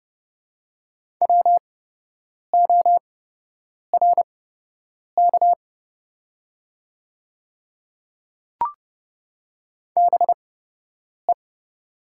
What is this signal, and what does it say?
Morse code, heard as a keyed tone, about 30 wpm, spaced as for 8 wpm, 690 Hz. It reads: WORK BE
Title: 8x Character Spacing - Top 200 Words - 30wpm